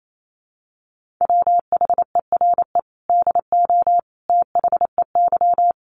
Morse code, heard as a keyed tone, about 28 wpm, 700 Hz. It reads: WHERE DO THEY
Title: Sentences from Top 200 Words 28wpm